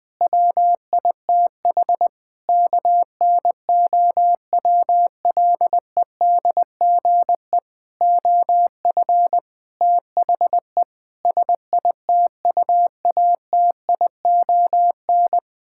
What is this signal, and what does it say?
Morse code, heard as a keyed tone, about 21 wpm, 700 Hz. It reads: WITH KNOWLEDGE OF THE SITUATION
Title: Sentences with 9-Word Phrases - 20wpm